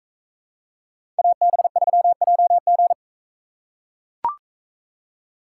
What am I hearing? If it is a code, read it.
Morse code, heard as a keyed tone, about 41 wpm, 700 Hz. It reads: AB3JG